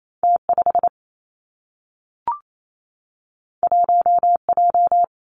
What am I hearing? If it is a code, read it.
Morse code, heard as a keyed tone, about 28 wpm, 700 Hz. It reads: T 5 1 J
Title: Two Challenging Letter-Number Combinations 28wpm